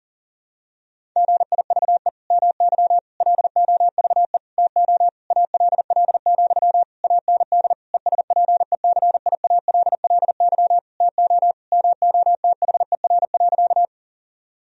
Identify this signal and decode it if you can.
Morse code, heard as a keyed tone, about 40 wpm, 700 Hz. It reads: GIVE MY LOVE TO ALL, AND ESPECIALLY TO MOTHER.
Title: Sentences from Top 1000 Words - Review Part 4 - 40wpm